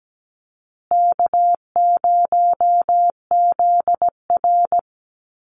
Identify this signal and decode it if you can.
Morse code, heard as a keyed tone, about 17 wpm, 700 Hz. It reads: K0ZR
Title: Easy Call Signs Repeated 2x - 17wpm